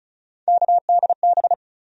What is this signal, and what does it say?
Morse code, heard as a keyed tone, about 36 wpm, 700 Hz. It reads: KDB
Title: USA Call Sign Postfixes - 3 Letters 35wpm